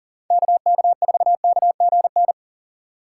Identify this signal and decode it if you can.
Morse code, heard as a keyed tone, about 39 wpm, 700 Hz. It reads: KK4KGN
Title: USA Call Signs - 3x Character Spacing - 40wpm